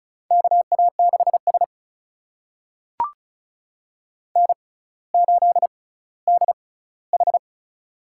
Morse code, heard as a keyed tone, about 36 wpm, spaced as for 18 wpm, 700 Hz. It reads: KA6S N8DH
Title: USA Call Signs - 4 Characters - 6x Character Spacing - 35wpm